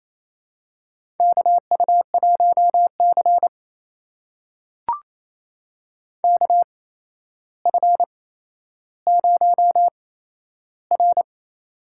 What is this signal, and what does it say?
Morse code, heard as a keyed tone, about 29 wpm, spaced as for 12 wpm, 700 Hz. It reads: KU1C KF0R
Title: USA Call Signs - 4 Characters - 8x Character Spacing - 28wpm